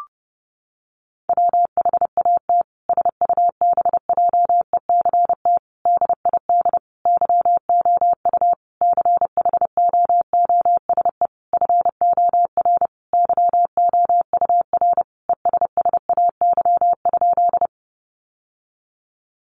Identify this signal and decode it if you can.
Morse code, heard as a keyed tone, about 30 wpm, 700 Hz. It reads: WHAT SUBJECT DID YOU CHOOSE FOR YOUR ESSAY?
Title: Sentences from Top 1500 Words 30wpm